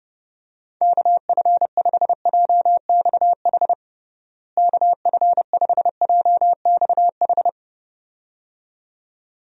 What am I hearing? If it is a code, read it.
Morse code, heard as a keyed tone, about 30 wpm, 700 Hz. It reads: KF5JXH KF5JXH
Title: USA Call Signs - 6 Characters - Repeated 2x - 30wpm